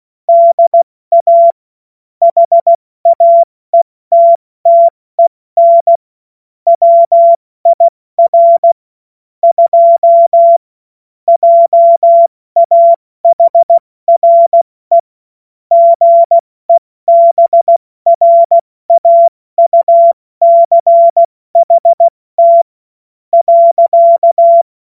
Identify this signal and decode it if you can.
Morse code, heard as a keyed tone, about 16 wpm, spaced as for 14 wpm, 680 Hz. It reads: DA HAETTEN WIR 2 JAHRE GEBRAUCHT.
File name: Morsekurs / Witze / Witz 30. Dezember 2025 70 F80